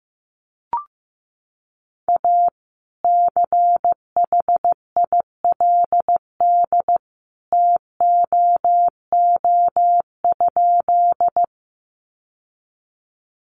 Morse code, heard as a keyed tone, about 15 wpm, 700 Hz. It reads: A CHILD TOO?
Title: Sentences from Top 800 Words 15wpm